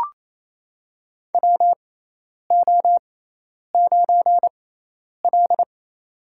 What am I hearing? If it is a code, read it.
Morse code, heard as a keyed tone, about 27 wpm, spaced as for 10 wpm, 700 Hz. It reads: WO9L
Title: USA Call Signs - 4 Characters - 6x Character Spacing - 28wpm